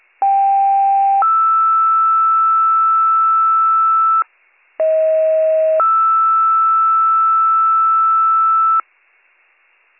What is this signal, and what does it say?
Two-tone page, heard as a one-second tone, then a three-second tone, about 790 then 1320 Hz, about 640 then 1320 Hz. It is heard from 0.2 to 8.8 s.